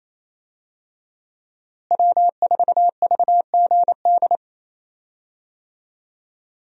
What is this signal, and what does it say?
Morse code, heard as a keyed tone, about 28 wpm, 700 Hz. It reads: W4VGD
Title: USA Call Signs - 5 Characters - Repeated 3x - 28wpm